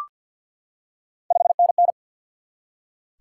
Morse code, heard as a keyed tone, about 52 wpm, 700 Hz. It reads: 5NN
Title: Top 90 QSO Elements 50wpm